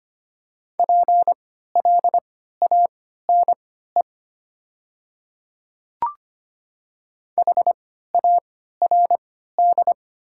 Morse code, heard as a keyed tone, about 25 wpm, spaced as for 14 wpm, 700 Hz. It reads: PLANE HARD
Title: 3x Character Spacing - Top 500 Words - 25wpm